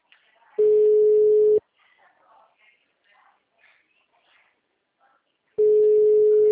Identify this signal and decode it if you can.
Ringback tone (Europe, 425 Hz), with speech.